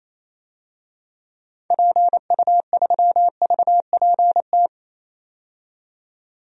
Morse code, heard as a keyed tone, about 27 wpm, 700 Hz. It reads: PU3VPT
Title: DXCC-100 Call Signs - 6 Characters - 28wpm